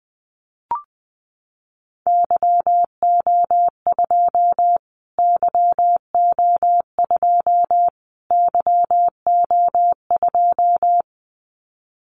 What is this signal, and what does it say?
Morse code, heard as a keyed tone, about 20 wpm, 700 Hz. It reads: YO2 YO2 YO2
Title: Callsign Prefixes for Top 25 DXCC - Repeated 3x - 20wpm